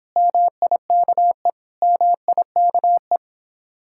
Morse code, heard as a keyed tone, about 25 wpm, 700 Hz. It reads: MIKE MIKE